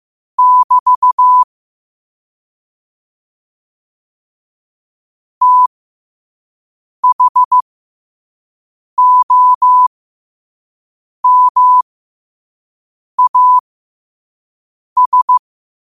Morse code, heard as a keyed tone, about 15 wpm, spaced as for 5 wpm, 1000 Hz. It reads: <BT> THOMAS
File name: qotdfiles / 5wpm